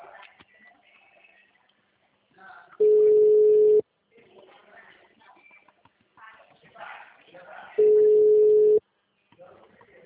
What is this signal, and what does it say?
Ringback tone (Europe, 425 Hz), with speech.